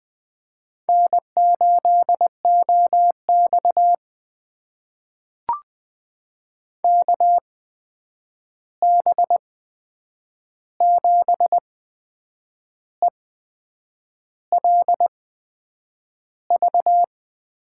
Morse code, heard as a keyed tone, about 20 wpm, spaced as for 8 wpm, 700 Hz. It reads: N8OX KB7ELV
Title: USA Call Signs - 8x Character Spacing - 20wpm